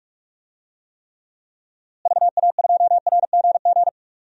Morse code, heard as a keyed tone, about 42 wpm, 700 Hz. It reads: VA2RGG
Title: DXCC-100 Call Signs - 6 Characters - Repeated 2x - 45wpm